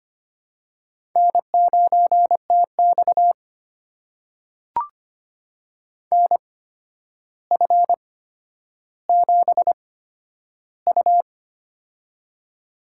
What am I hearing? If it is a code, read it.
Morse code, heard as a keyed tone, about 25 wpm, spaced as for 11 wpm, 700 Hz. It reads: N9TX NF7U
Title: USA Call Signs - 4 Characters - 8x Character Spacing - 25wpm